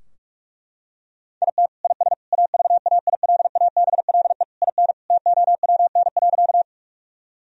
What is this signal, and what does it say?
Morse code, heard as a keyed tone, about 41 wpm, 700 Hz. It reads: IT IS AVAILABLE IN TOWN.
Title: Sentences from Top 1000 Words - Review Part 3 - 45wpm